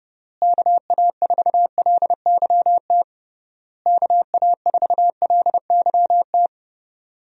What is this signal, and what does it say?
Morse code, heard as a keyed tone, about 32 wpm, 700 Hz. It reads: KA4LYT KA4LYT
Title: USA Call Signs - 6 Characters - Repeated 3x - 30wpm